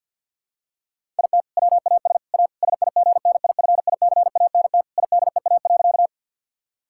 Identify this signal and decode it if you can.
Morse code, heard as a keyed tone, about 50 wpm, 700 Hz. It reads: IT WAS A SIGNIFICANT IDEA.
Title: Sentences from Top 1000 Words - Review Part 6 - 50wpm